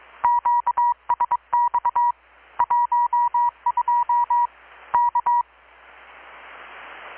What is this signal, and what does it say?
Morse code, heard as a keyed tone, about 23 wpm, 960 Hz. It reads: QSX 12 K
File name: blog / swl / 6-abril-2019-12695-USB-KPS